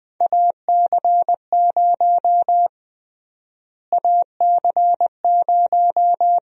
Morse code, heard as a keyed tone, about 20 wpm, 700 Hz. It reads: AC0 AC0
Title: USA Call Sign Prefixes - Repeated 2x - 20wpm